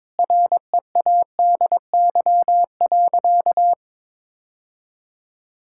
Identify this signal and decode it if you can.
Morse code, heard as a keyed tone, about 22 wpm, 700 Hz. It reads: READY.
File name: practice / 22wpm / Sentences with 3-Word Phrases - 22wpm